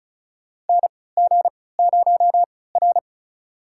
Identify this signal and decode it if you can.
Morse code, heard as a keyed tone, about 36 wpm, spaced as for 20 wpm, 700 Hz. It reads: NG0R
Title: USA Call Signs - 3x Character Spacing - 35wpm